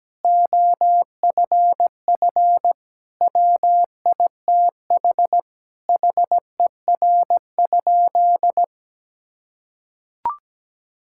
Morse code, heard as a keyed tone, about 17 wpm, 700 Hz. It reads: OFF WITH HER?
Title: Sentences from Top 1000 Words - Review Part 6 - 17wpm